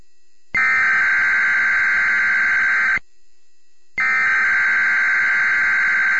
Emergency Alert System (SAME) tones (header bursts), heard as harsh buzzing bursts. It is heard from 0.6 to 5.2 s.